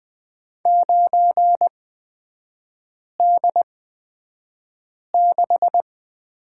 Morse code, heard as a keyed tone, about 20 wpm, spaced as for 5 wpm, 700 Hz. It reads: 9D6
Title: 243 Lesson 39e - Letter z - Three Characters 20wpm